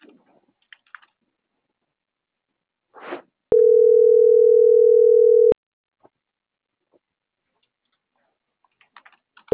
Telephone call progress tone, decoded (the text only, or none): ringback tone